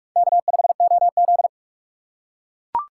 Morse code, heard as a keyed tone, about 44 wpm, 700 Hz. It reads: K5OZ